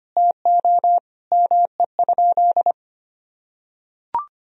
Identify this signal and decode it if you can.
Morse code, heard as a keyed tone, about 25 wpm, 700 Hz. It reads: TO ME?
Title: Sentences from Top 500 Words and Abbreviations 25wpm